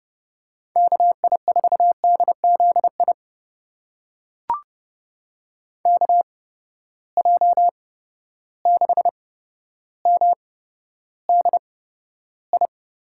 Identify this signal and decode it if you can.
Morse code, heard as a keyed tone, about 31 wpm, spaced as for 13 wpm, 700 Hz. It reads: KI4DZI KJ6MDI
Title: USA Call Signs - 6 Characters - 8x Character Spacing - 30wpm